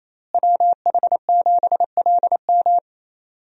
Morse code, heard as a keyed tone, about 28 wpm, 700 Hz. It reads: WH7LM
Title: USA Call Signs - 5 Characters - Repeated 3x - 28wpm